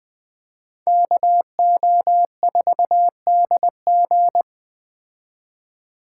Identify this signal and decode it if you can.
Morse code, heard as a keyed tone, about 20 wpm, 700 Hz. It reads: KO4DG
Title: Speed Racing - USA 2x2 Call Signs 17wpm